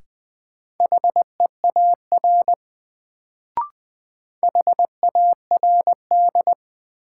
Morse code, heard as a keyed tone, about 20 wpm, 700 Hz. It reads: HEAR HARD
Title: Sentences from Top 300 Words - New Word Drill - 20wpm